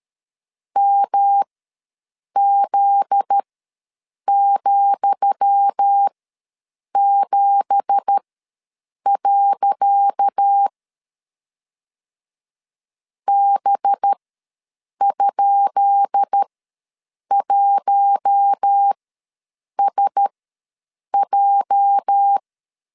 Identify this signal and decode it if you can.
Morse code, heard as a keyed tone, about 13 wpm, spaced as for 7 wpm, 800 Hz. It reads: MZ,7. B?1SJ
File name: LEZIONI / 10 SIGNS / 10D COMA ws 12wpm 3618